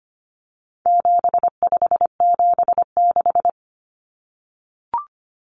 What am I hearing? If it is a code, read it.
Morse code, heard as a keyed tone, about 25 wpm, 700 Hz. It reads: 7576
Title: Common Cut Numbers - 4-Digit 25wpm